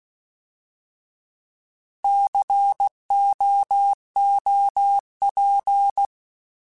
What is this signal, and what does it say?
Morse code, heard as a keyed tone, about 16 wpm, 800 Hz. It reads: COOP
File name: Alphabetical / Alphabetical mp3s / Cs @15